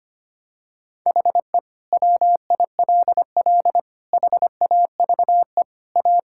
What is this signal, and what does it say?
Morse code, heard as a keyed tone, about 25 wpm, 700 Hz. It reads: HE WILL HAVE A
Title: Sentences from Top 1000 Words - Review Part 6 - 25wpm